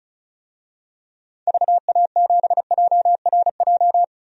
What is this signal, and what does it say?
Morse code, heard as a keyed tone, about 35 wpm, 700 Hz. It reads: VA7JRJ
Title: DXCC-100 Call Signs - 6 Characters - 35wpm